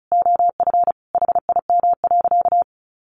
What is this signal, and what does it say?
Morse code, heard as a keyed tone, about 34 wpm, 700 Hz. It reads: OF HIM.
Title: Sentences from Top 1300 Words 35wpm